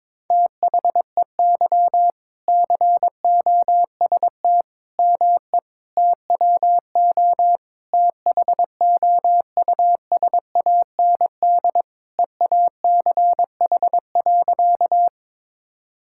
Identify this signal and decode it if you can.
Morse code, heard as a keyed tone, about 22 wpm, 700 Hz. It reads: THEY COST ME TWO THOUSAND EACH.
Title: Sentences from Top 1000 Words - Review Part 5 - 22wpm